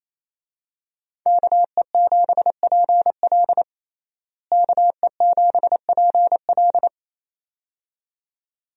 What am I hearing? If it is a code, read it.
Morse code, heard as a keyed tone, about 28 wpm, 700 Hz. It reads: KE7PL KE7PL
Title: USA Call Signs - 5 Characters - Repeated 2x - 28wpm